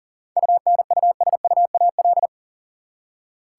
Morse code, heard as a keyed tone, about 40 wpm, 700 Hz. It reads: UNUSUAL
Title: Seven Letter Words - 40wpm